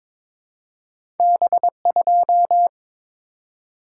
Morse code, heard as a keyed tone, about 22 wpm, 700 Hz. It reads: B2
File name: practice / 22wpm / Two Letter-Number Combinations - 22wpm